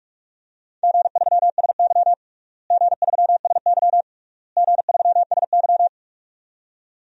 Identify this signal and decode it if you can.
Morse code, heard as a keyed tone, about 46 wpm, 700 Hz. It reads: G3SY G3SY G3SY
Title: DXCC-100 Call Signs - 4 Characters - Repeated 3x - 45wpm